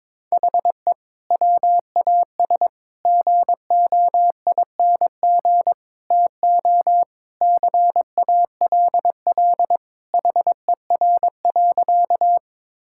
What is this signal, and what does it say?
Morse code, heard as a keyed tone, about 22 wpm, 700 Hz. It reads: HE WAS GOING TO CALL HER.
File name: practice / 22wpm / Sentences with 4-Word Phrases - 22wpm